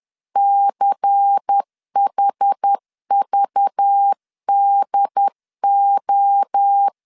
Morse code, heard as a keyed tone, about 10 wpm, 800 Hz. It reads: CHVDO
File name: LEZIONI / 6 PKX0 / 6 PK 10wpm 2436